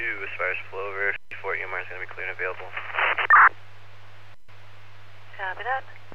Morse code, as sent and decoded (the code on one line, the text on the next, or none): none